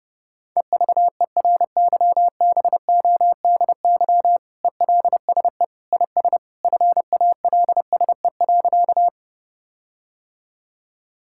Morse code, heard as a keyed tone, about 30 wpm, 700 Hz. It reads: EVERYBODY ELSE IS FALSE.
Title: Sentences from Top 1600 Words 30wpm